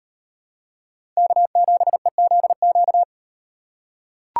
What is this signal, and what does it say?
Morse code, heard as a keyed tone, about 36 wpm, 700 Hz. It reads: K7EZQ